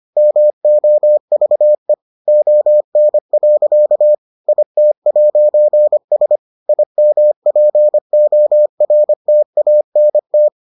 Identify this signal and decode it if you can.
Morse code, heard as a keyed tone, about 25 wpm, 600 Hz. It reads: MOVE ON. IT'S IMPORTANT